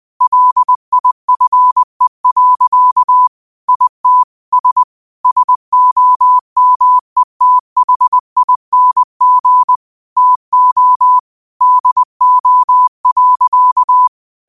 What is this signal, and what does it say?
Morse code, heard as a keyed tone, about 20 wpm, 1000 Hz. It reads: LIFE. IT S SOMETHING TO DO.
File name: qotdfiles / 20wpm